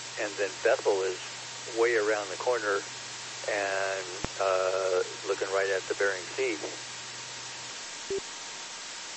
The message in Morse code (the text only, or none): none